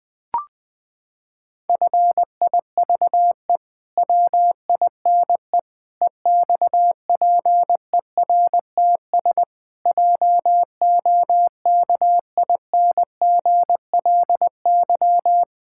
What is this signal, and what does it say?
Morse code, heard as a keyed tone, about 19 wpm, 700 Hz. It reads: FIVE WINE EXPERTS JOKINGLY
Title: Pangram Challenge v2 - 20wpm